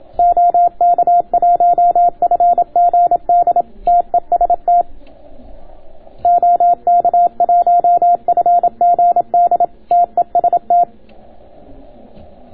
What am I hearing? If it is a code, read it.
Morse code, heard as a keyed tone, about 27 wpm, 700 Hz. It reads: OK1FGD TEST OK1FGD TEST